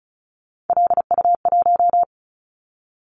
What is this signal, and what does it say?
Morse code, heard as a keyed tone, about 36 wpm, 700 Hz. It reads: LU1
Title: Callsign Prefixes for Top 50 DXCC - 35wpm